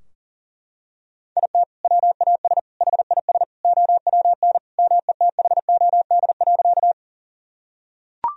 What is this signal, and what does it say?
Morse code, heard as a keyed tone, about 37 wpm, 700 Hz. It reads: IT WAS HIS OWN METHOD.
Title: Sentences from Top 1000 Words - Review Part 3 - 40wpm